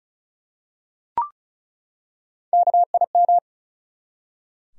Morse code, heard as a keyed tone, about 35 wpm, 700 Hz. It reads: KIM
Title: USA Common Names - Max 4 Letters - 35wpm